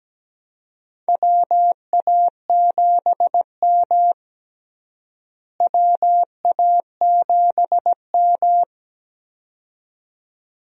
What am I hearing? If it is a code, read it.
Morse code, heard as a keyed tone, about 17 wpm, 700 Hz. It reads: WA7M WA7M
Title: USA Call Signs - 4 Characters - Repeated 2x - 17wpm